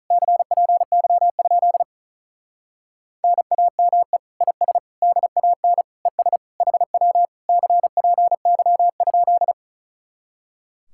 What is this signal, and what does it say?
Morse code, heard as a keyed tone, about 43 wpm, 700 Hz. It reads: CPY? NAME IS DAN ES HW CPY?
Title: Speed-Racing Common QSO Phrases - 35wpm